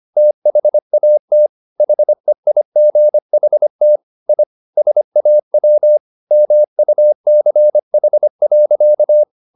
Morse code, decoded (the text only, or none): THAT HEIGHT I SAW MUCH.